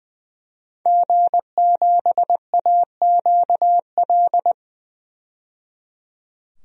Morse code, heard as a keyed tone, about 21 wpm, 700 Hz. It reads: G7AQL
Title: DXCC-100 Call Signs - 5 Characters - 20wpm